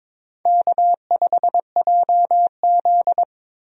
Morse code, heard as a keyed tone, about 22 wpm, 700 Hz. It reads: K5JZ
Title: USA Call Signs - 4 Characters - Repeated 2x - 22wpm